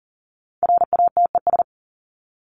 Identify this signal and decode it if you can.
Morse code, heard as a keyed tone, about 40 wpm, 700 Hz. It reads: RATES
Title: ES:S Suffix - Top 100 Words with Base Words 40wpm